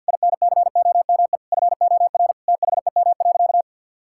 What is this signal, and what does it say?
Morse code, heard as a keyed tone, about 46 wpm, 700 Hz. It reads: INCOME FOR THEM.
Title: Sentences from Top 1000 Words - Review Part 5 - 50wpm